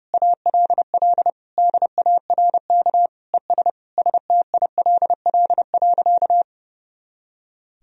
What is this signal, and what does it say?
Morse code, heard as a keyed tone, about 30 wpm, 700 Hz. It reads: ALL DARK ES STILL.
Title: Sentences from Top 500 Words and Abbreviations 30wpm